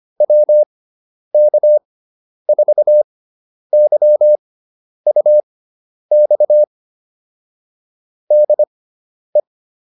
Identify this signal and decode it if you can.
Morse code, heard as a keyed tone, about 25 wpm, spaced as for 10 wpm, 600 Hz. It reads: WK4YUX DE